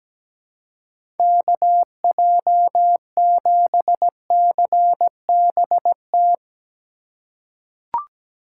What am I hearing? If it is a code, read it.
Morse code, heard as a keyed tone, about 17 wpm, 700 Hz. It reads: KJ7CBT
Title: USA Call Signs - 6 Characters - 4x Character Spacing - 17wpm